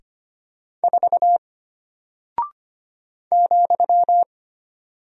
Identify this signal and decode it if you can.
Morse code, heard as a keyed tone, about 25 wpm, 700 Hz. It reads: 4 ,
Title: Single Character (Letter:Number:Punctuation:Prosign) Alt 25wpm